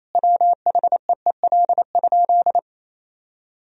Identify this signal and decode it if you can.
Morse code, heard as a keyed tone, about 28 wpm, 700 Hz. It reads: WHEEL?
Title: Sentences from Top 1000 Words - Review Part 3 - 28wpm